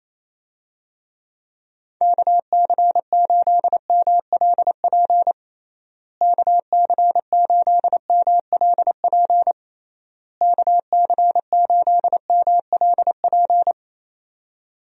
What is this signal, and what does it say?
Morse code, heard as a keyed tone, about 27 wpm, 700 Hz. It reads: KC8MLP KC8MLP KC8MLP
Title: USA Call Signs - 6 Characters - Repeated 3x - 28wpm